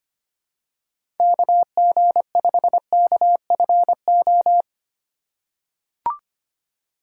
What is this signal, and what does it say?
Morse code, heard as a keyed tone, about 25 wpm, 700 Hz. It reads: KG5KFO
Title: USA Call Signs - 6 Characters - 3x Character Spacing - 25wpm